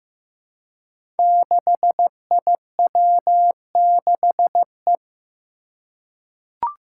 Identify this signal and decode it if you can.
Morse code, heard as a keyed tone, about 15 wpm, 700 Hz. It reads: 6IW6E
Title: Five Challenging Letter-Number Combinations 15wpm